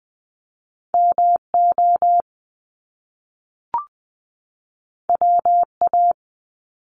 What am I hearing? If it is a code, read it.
Morse code, heard as a keyed tone, about 19 wpm, 700 Hz. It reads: MO WA
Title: Top Two Letter Combinations in Words 20wpm